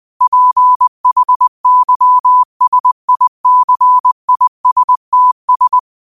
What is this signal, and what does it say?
Morse code, heard as a keyed tone, about 20 wpm, 1000 Hz. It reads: PHYSICISTS